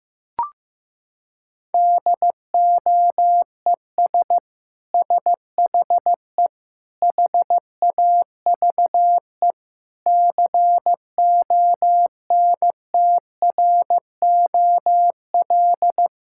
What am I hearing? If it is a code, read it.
Morse code, heard as a keyed tone, about 15 wpm, 700 Hz. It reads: DOES SHE HAVE CONTROL
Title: Sentences from Top 1000 Words - Review Part 5 - 15wpm